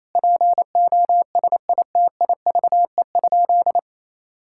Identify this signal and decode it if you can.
Morse code, heard as a keyed tone, about 28 wpm, 700 Hz. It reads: POSITIVE?
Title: Sentences from Top 1000 Words - Review Part 5 - 28wpm